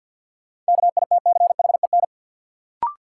Morse code, heard as a keyed tone, about 52 wpm, 700 Hz. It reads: KITCHEN
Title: Sentences from Top 1400 Words - New Word Drill - 50wpm